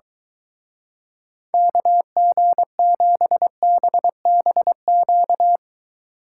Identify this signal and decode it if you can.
Morse code, heard as a keyed tone, about 23 wpm, 700 Hz. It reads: KG7BBQ